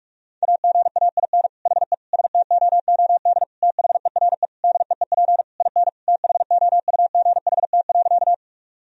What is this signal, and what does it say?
Morse code, heard as a keyed tone, about 42 wpm, 700 Hz. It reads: AGAIN HE STOOD THERE DEEP IN THOUGHT.
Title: Sentences from Top 1000 Words - Review Part 3 - 45wpm